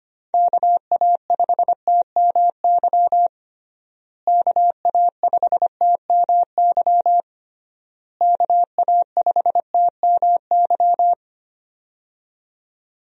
Morse code, heard as a keyed tone, about 25 wpm, 700 Hz. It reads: KA5TMY KA5TMY KA5TMY
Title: USA Call Signs - 6 Characters - Repeated 3x - 25wpm